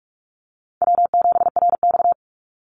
Morse code, heard as a keyed tone, about 44 wpm, 700 Hz. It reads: W7RX
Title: Easy Call Signs Repeated 2x - 45wpm